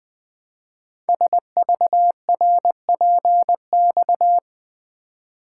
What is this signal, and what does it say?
Morse code, heard as a keyed tone, about 20 wpm, 700 Hz. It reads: SVRPX
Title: Code Groups - 5 Letters - 20wpm